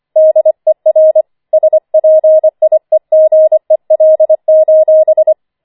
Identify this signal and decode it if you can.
Morse code, heard as a keyed tone, about 25 wpm, 630 Hz. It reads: DER SPIEGEL:
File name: 25wpm